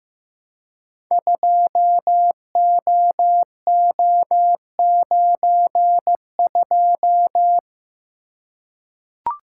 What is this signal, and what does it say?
Morse code, heard as a keyed tone, about 15 wpm, 700 Hz. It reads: 2OO92
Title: Code Groups - 5 Letter:Numbers - 15wpm